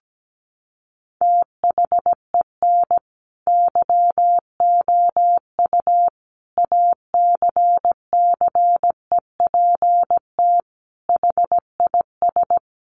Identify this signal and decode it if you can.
Morse code, heard as a keyed tone, about 17 wpm, 700 Hz. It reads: THEN YOU ACCEPT HIS
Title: Sentences from Top 1300 Words 17wpm